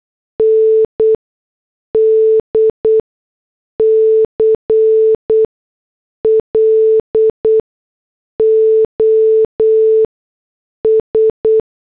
Morse code, heard as a keyed tone, about 8 wpm, spaced as for 6 wpm, 440 Hz. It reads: ND CLOS